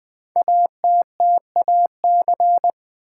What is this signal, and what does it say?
Morse code, heard as a keyed tone, about 20 wpm, 700 Hz. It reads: ATTAC
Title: Build A Word - 5 Letters At A Time - 10-Letter Words - 20wpm